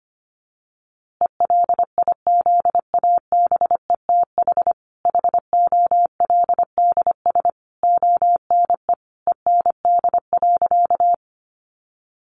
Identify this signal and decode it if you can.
Morse code, heard as a keyed tone, about 25 wpm, 700 Hz. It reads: ELIZABETH HOLDS ONE END.
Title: Easy Phrases 25wpm